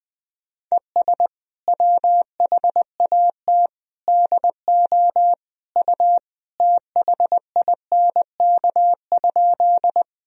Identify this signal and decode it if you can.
Morse code, heard as a keyed tone, about 20 wpm, 700 Hz. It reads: ES WHAT DO U THINK?
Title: Sentences from Top 200 Words and Abbreviations 20wpm